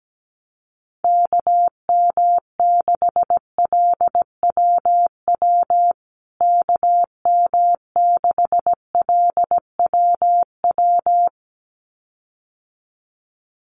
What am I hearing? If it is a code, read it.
Morse code, heard as a keyed tone, about 17 wpm, 700 Hz. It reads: KM6LWW KM6LWW
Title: USA Call Signs Repeated 2x 17wpm